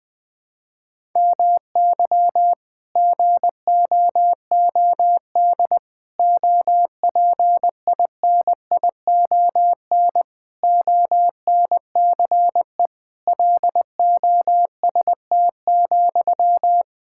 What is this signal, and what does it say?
Morse code, heard as a keyed tone, about 20 wpm, 700 Hz. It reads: MY GOOD OPINION ONCE LOST,
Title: Sentences from Top 1600 Words 20wpm